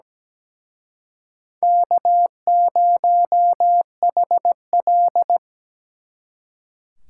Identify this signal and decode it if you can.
Morse code, heard as a keyed tone, about 20 wpm, 700 Hz. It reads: K0HL
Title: Speed Racing - USA 1x2 Call Signs 17wpm